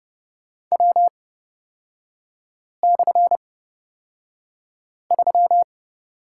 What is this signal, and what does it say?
Morse code, heard as a keyed tone, about 30 wpm, spaced as for 5 wpm, 700 Hz. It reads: W/3
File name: Lesson 38e - Letter q - Three Characters 30wpm